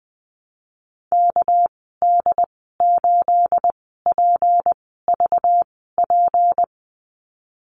Morse code, heard as a keyed tone, about 20 wpm, spaced as for 15 wpm, 700 Hz. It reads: KD8PVP
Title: 219 Lesson 35g - Number 0 - Call Signs 20wpm